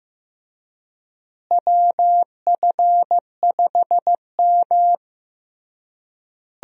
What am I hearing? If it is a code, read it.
Morse code, heard as a keyed tone, about 15 wpm, 700 Hz. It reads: WF5M
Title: USA Call Signs - 4 Characters - Repeated 3x - 15wpm